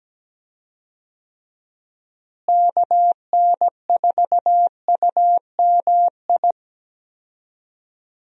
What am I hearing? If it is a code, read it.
Morse code, heard as a keyed tone, about 17 wpm, 700 Hz. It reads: KN4UMI
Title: USA Call Signs - 6 Characters - Repeated 3x - 17wpm